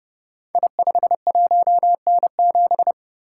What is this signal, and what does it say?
Morse code, heard as a keyed tone, about 30 wpm, 700 Hz. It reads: I51N7
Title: Code Groups - 5 Letter:Numbers - 30wpm